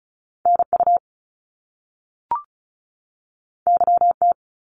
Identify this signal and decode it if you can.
Morse code, heard as a keyed tone, about 36 wpm, 700 Hz. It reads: NU YT
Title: Canadian Provinces and Territories 35wpm